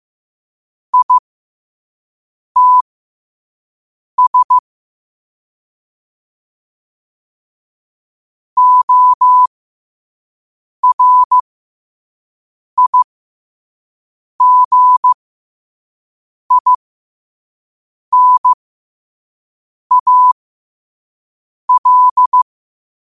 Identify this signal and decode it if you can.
Morse code, heard as a keyed tone, about 15 wpm, spaced as for 5 wpm, 1000 Hz. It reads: ITS ORIGINAL
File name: qotdfiles / 5wpm